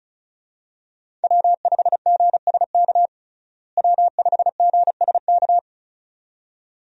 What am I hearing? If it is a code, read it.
Morse code, heard as a keyed tone, about 34 wpm, 700 Hz. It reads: W5GSK W5GSK
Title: USA Call Signs - 5 Characters - Repeated 2x - 35wpm